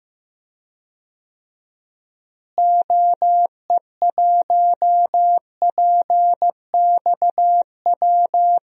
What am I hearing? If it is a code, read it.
Morse code, heard as a keyed tone, about 15 wpm, 700 Hz. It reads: OE1PXW
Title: DXCC-100 Call Signs - 6 Characters - Repeated 2x - 15wpm